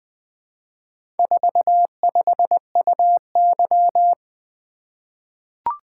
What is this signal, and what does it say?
Morse code, heard as a keyed tone, about 20 wpm, 700 Hz. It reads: 45UY
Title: Four Letter-Number Combinations - 20wpm